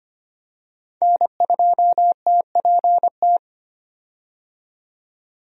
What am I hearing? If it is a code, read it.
Morse code, heard as a keyed tone, about 25 wpm, 700 Hz. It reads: N2TPT
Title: USA Call Signs - 5 Characters - 25wpm